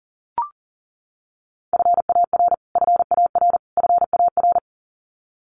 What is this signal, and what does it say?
Morse code, heard as a keyed tone, about 40 wpm, 700 Hz. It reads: FAR FAR FAR
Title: Top 500 Words Repeated 40wpm